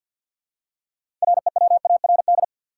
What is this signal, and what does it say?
Morse code, heard as a keyed tone, about 46 wpm, 700 Hz. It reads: REWARD
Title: Six Letter Words - 50wpm